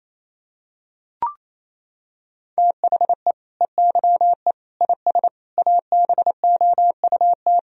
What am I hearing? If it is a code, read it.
Morse code, heard as a keyed tone, about 28 wpm, 700 Hz. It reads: THE EYE IS ABOUT
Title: Sentences from Top 1000 Words - Review Part 3 - 28wpm